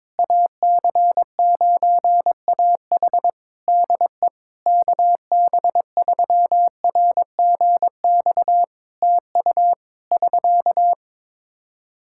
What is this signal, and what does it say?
Morse code, heard as a keyed tone, about 22 wpm, 700 Hz. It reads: AC9AH DE KB3RGX TU <SK>
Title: Practice QSO - Exchange 4 - Ending - 22wpm